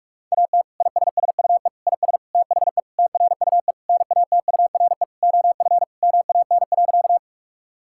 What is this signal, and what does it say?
Morse code, heard as a keyed tone, about 48 wpm, 700 Hz. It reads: AT ISSUE IS THE TRUE NATURE OF MAN.
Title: Sentences from Top 1000 Words - Review Part 6 - 45wpm